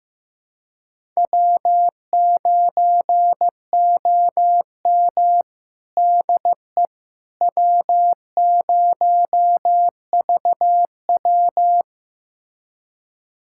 Morse code, heard as a keyed tone, about 15 wpm, 700 Hz. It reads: W9OM DE W0VW
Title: USA Callsign DE Callsign - 4 Characters - 15wpm